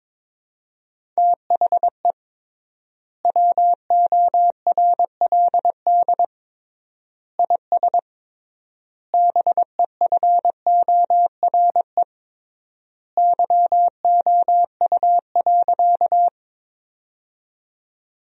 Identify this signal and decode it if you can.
Morse code, heard as a keyed tone, about 22 wpm, 700 Hz. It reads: THE WORLD IS BEFORE YOU.
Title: 3x Word-Spacing - Sentences from Top 200 Words - 22wpm